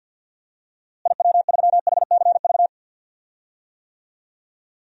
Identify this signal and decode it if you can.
Morse code, heard as a keyed tone, about 51 wpm, 700 Hz. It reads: IW3HCV
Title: DXCC-100 Call Signs - 6 Characters - Repeated 2x - 50wpm